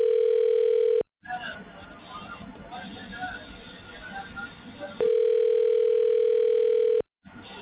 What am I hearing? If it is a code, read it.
Ringback tone (North America), 440 + 480 Hz, with speech.
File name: recordings / RE7ebd65acc3a352d2a3bc c59957